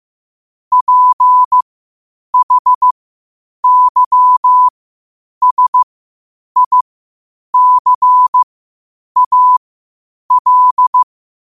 Morse code, heard as a keyed tone, about 15 wpm, spaced as for 8 wpm, 1000 Hz. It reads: PHYSICAL
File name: qotdfiles / 8wpm